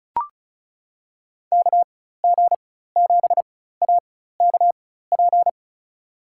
Morse code, heard as a keyed tone, about 34 wpm, spaced as for 16 wpm, 690 Hz. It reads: KG7AKP
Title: USA Call Signs - 6 Characters - 4x Character Spacing - 35wpm